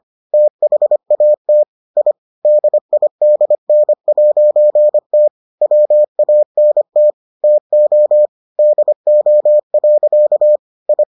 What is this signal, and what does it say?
Morse code, heard as a keyed tone, about 25 wpm, 600 Hz. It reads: THAT I DIDN'T WANT TO DO. I